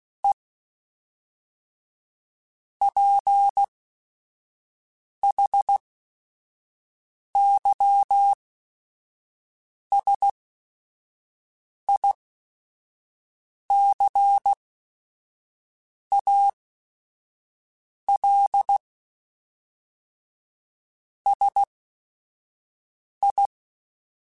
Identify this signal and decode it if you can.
Morse code, heard as a keyed tone, about 16 wpm, spaced as for 5 wpm, 800 Hz. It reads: E PHYSICAL SI
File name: Technical Subjects / Technical Subjects-mp3s / Sw Pwr Sup @5